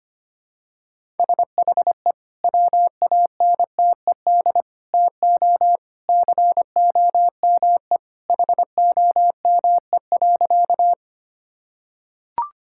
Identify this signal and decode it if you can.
Morse code, heard as a keyed tone, about 25 wpm, 700 Hz. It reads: SHE WANTED TO COME HOME.
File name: practice / 25wpm / Sentences with 2-Word Phrases - 25wpm